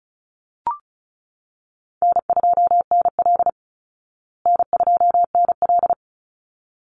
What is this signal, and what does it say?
Morse code, heard as a keyed tone, about 34 wpm, 700 Hz. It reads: N2NL N2NL
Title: Call Signs Repeated 3x - 35wpm